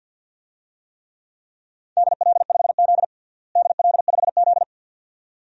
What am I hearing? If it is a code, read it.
Morse code, heard as a keyed tone, about 52 wpm, 700 Hz. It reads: DL5Z DL5Z
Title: DXCC-100 Call Signs - 4 Characters - Repeated 2x - 50wpm